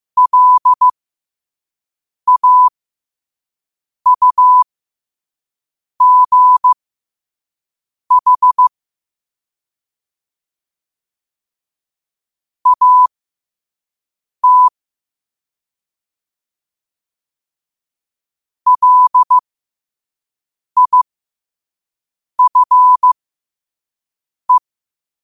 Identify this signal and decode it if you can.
Morse code, heard as a keyed tone, about 15 wpm, spaced as for 5 wpm, 1000 Hz. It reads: LAUGH AT LIFE
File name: 2024-10-07-5wpm